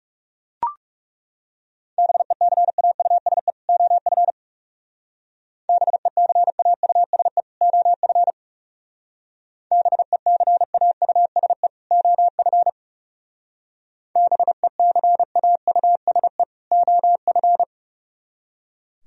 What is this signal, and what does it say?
Morse code, heard as a keyed tone, about 37 wpm, 700 Hz. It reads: BECAUSE OF BECAUSE OF BECAUSE OF BECAUSE OF
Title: Speed-Racing - Sets of 2 Words 30wpm